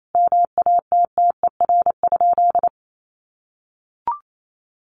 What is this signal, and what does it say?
Morse code, heard as a keyed tone, about 28 wpm, 700 Hz. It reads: MATTER?
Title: Sentences from Top 700 Words 28wpm